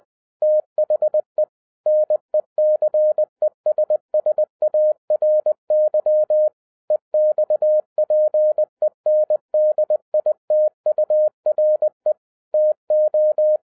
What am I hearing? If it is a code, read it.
Morse code, heard as a keyed tone, about 20 wpm, 610 Hz. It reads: THE NECESSARY EXPENDITURE TO